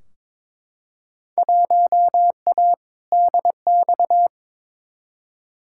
Morse code, heard as a keyed tone, about 22 wpm, 700 Hz. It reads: 1A DX
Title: Field Day Exchanges Encore - 22wpm